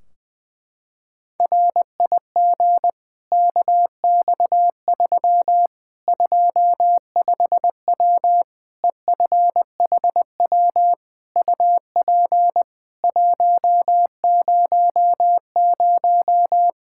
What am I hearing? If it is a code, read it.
Morse code, heard as a keyed tone, about 20 wpm, 700 Hz. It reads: RIG KX3 25W EFHW UP 100
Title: Practice QSO - Exchange 2 - Info - 20wpm